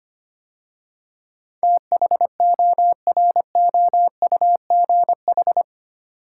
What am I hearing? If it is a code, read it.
Morse code, heard as a keyed tone, about 25 wpm, 700 Hz. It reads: THOROUGH
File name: 8 Letter Words - First 3 Letters Given - 25wpm